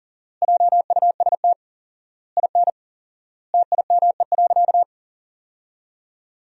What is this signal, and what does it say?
Morse code, heard as a keyed tone, about 40 wpm, 700 Hz. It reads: JUST IN TIME.
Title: 4x Word-Spacing - Sentences from Top 300 Words - 40wpm